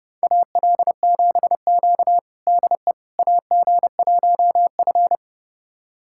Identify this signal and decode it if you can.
Morse code, heard as a keyed tone, about 31 wpm, 700 Hz. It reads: AL7Q DE AG1F
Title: USA Callsign DE Callsign - 4 Characters - 30wpm